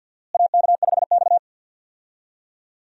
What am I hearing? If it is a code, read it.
Morse code, heard as a keyed tone, about 49 wpm, 700 Hz. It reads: AK5X
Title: USA Call Signs - 4 Characters - Repeated 2x - 50wpm